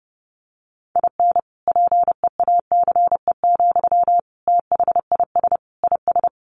Morse code, heard as a keyed tone, about 30 wpm, 700 Hz. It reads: IN PEACE, THIS IS
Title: Sentences from Top 1500 Words 30wpm